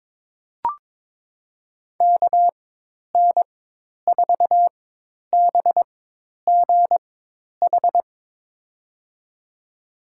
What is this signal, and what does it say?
Morse code, heard as a keyed tone, about 22 wpm, spaced as for 10 wpm, 700 Hz. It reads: KN4BGH
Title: USA Call Signs - 6 Characters - 4x Character Spacing - 22wpm